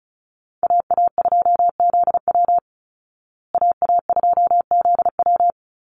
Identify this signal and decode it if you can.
Morse code, heard as a keyed tone, about 34 wpm, 700 Hz. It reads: AA2ZW AA2ZW
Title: Call Signs Repeated 2x - 35wpm